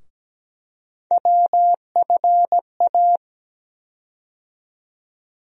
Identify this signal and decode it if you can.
Morse code, heard as a keyed tone, about 17 wpm, 700 Hz. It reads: WFA
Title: USA Call Sign Postfixes - 3 Letters 17wpm